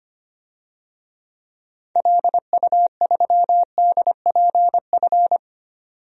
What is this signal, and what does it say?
Morse code, heard as a keyed tone, about 25 wpm, 700 Hz. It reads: LU3DPF
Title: DXCC-100 Call Signs - 6 Characters - 25wpm